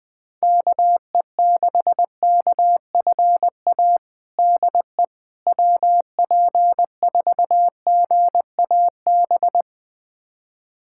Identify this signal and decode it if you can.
Morse code, heard as a keyed tone, about 20 wpm, 700 Hz. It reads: KE6KFA DE WP4GAB
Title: USA Callsign DE Callsign - 6 Characters - 20wpm